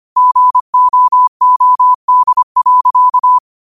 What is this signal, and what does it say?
Morse code, heard as a keyed tone, about 25 wpm, 1000 Hz. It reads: GOOD.
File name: 25wpm